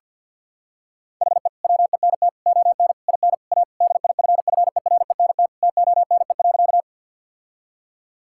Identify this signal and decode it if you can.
Morse code, heard as a keyed tone, about 52 wpm, 700 Hz. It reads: HE WENT ON IN A DIFFERENT TONE.
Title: Sentences from Top 1600 Words 50wpm